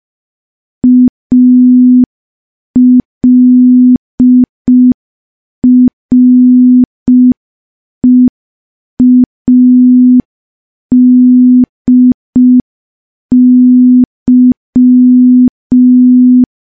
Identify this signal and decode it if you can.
Morse code, heard as a keyed tone, about 5 wpm, 250 Hz. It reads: ALREADY
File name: morse / morse-training-05WPM